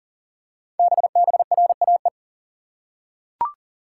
Morse code, heard as a keyed tone, about 39 wpm, 700 Hz. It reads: BBRAE
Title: Code Groups - 5 Letters - 40wpm